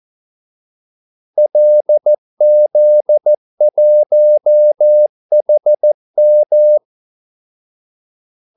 Morse code, heard as a keyed tone, about 14 wpm, 600 Hz. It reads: LZ1HM
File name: Callsigns-14wpm